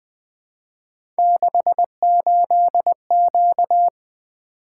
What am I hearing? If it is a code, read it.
Morse code, heard as a keyed tone, about 21 wpm, 700 Hz. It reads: 68Q
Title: Three Letter-Number Combinations - 20wpm